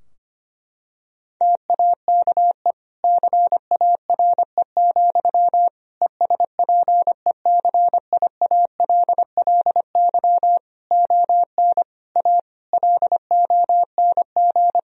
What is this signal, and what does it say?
Morse code, heard as a keyed tone, about 25 wpm, 700 Hz. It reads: TAKE CARE, ESPECIALLY ON A LONG
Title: Sentences from Top 1000 Words - Review Part 4 - 25wpm